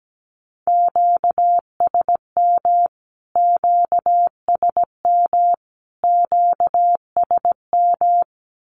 Morse code, heard as a keyed tone, about 17 wpm, 700 Hz. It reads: QSM QSM QSM
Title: Q-Codes Repeated 3x 17wpm